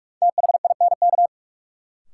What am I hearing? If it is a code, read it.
Morse code, heard as a keyed tone, about 45 wpm, 700 Hz. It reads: THINK
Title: Sets of 2 Words - New Word Drill - 45wpm